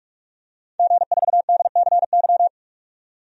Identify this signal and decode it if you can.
Morse code, heard as a keyed tone, about 46 wpm, 700 Hz. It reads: G4DCY
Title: DXCC-100 Call Signs - 5 Characters - 45wpm